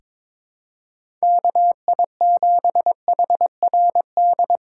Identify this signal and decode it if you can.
Morse code, heard as a keyed tone, about 22 wpm, 700 Hz. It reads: KI7HRD
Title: USA Call Signs - 6 Characters - 4x Character Spacing - 22wpm